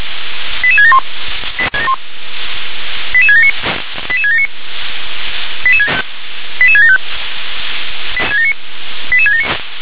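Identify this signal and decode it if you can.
Five-tone selective call (ZVEI), heard as quick runs of five tones, tones of about 70 ms. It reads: 88671 88679 88679 88675 88679
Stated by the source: ZVEI five-tone selective call received with an SDR.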